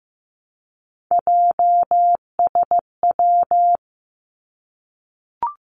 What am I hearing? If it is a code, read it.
Morse code, heard as a keyed tone, about 15 wpm, 700 Hz. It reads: JSW
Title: Three Challenging Letter-Number Combinations 15wpm